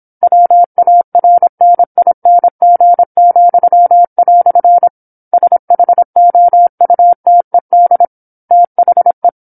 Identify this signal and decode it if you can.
Morse code, heard as a keyed tone, about 27 wpm, 700 Hz. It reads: WARNING," SHOUTED THE
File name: 08-Chapter-08 26 WPM